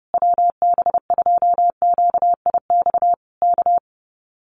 Morse code, heard as a keyed tone, about 30 wpm, 700 Hz. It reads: WB2QIX K
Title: Common QSO Phrases Encore 30wpm